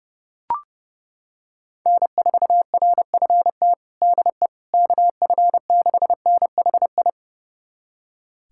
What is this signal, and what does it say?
Morse code, heard as a keyed tone, about 30 wpm, 700 Hz. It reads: N4RFT DE KF6NHI
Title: USA Callsign DE Callsign 30wpm